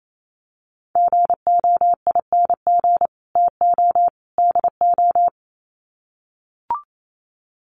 Morse code, heard as a keyed tone, about 28 wpm, 700 Hz. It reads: GOING TO DO
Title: Sets of 3 Words 28wpm